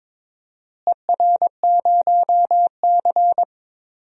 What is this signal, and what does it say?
Morse code, heard as a keyed tone, about 22 wpm, 700 Hz. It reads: ER0C
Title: DXCC-100 Call Signs - 4 Characters - Repeated 3x - 22wpm